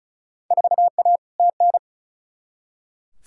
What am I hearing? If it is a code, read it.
Morse code, heard as a keyed tone, about 36 wpm, 690 Hz. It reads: 4A TN
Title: Field Day Exchanges Encore - 35wpm